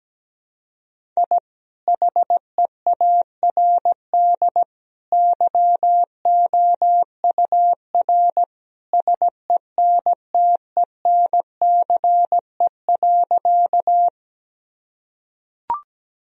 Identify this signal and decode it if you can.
Morse code, heard as a keyed tone, about 17 wpm, 700 Hz. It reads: I HEARD YOUR SENTENCE.
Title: Sentences from Top 1000 Words - Review Part 1 - 17wpm